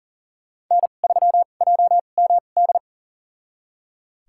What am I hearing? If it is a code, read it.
Morse code, heard as a keyed tone, about 40 wpm, spaced as for 29 wpm, 700 Hz. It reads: N3JMD
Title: USA Call Signs - 5 Characters - 2x Character Spacing - 40wpm